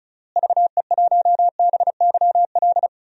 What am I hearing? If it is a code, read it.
Morse code, heard as a keyed tone, about 36 wpm, 700 Hz. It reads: VE1BYL
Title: DXCC-100 Call Signs - 6 Characters - Repeated 3x - 35wpm